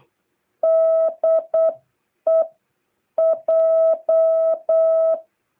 Morse code, heard as a keyed tone, about 8 wpm, 660 Hz. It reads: DE J